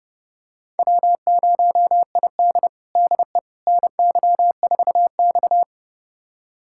Morse code, heard as a keyed tone, about 31 wpm, 700 Hz. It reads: W0ID DE NY4X
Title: USA Callsign DE Callsign - 4 Characters - 30wpm